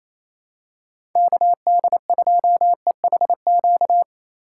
Morse code, heard as a keyed tone, about 28 wpm, 700 Hz. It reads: KD2EHQ
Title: USA Call Signs - 6 Characters - 4x Character Spacing - 28wpm